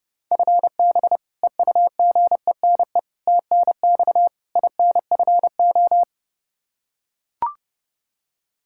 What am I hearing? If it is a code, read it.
Morse code, heard as a keyed tone, about 30 wpm, 700 Hz. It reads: FB EUGENE TNX INFO
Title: Practice QSO - Exchange 3 - More Info - 30wpm